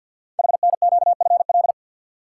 Morse code, heard as a keyed tone, about 48 wpm, 700 Hz. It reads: HNQFL
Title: Code Groups - 5 Letters - 50wpm